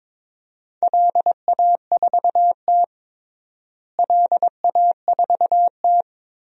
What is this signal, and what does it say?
Morse code, heard as a keyed tone, about 22 wpm, 700 Hz. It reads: LA4T LA4T
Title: DXCC-100 Call Signs - 4 Characters - Repeated 3x - 22wpm